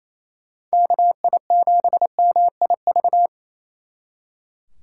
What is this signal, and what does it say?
Morse code, heard as a keyed tone, about 36 wpm, 700 Hz. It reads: KI7MIV